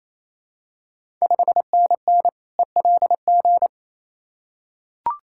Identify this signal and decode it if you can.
Morse code, heard as a keyed tone, about 28 wpm, 690 Hz. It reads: 5NN ELG
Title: Ontario QSO Party Exchange 28wpm